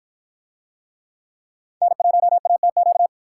Morse code, heard as a keyed tone, about 51 wpm, 700 Hz. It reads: N1ATQ